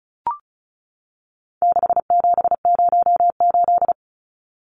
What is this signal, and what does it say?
Morse code, heard as a keyed tone, about 34 wpm, 700 Hz. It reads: 6708
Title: Numbers - 4-Digit 35wpm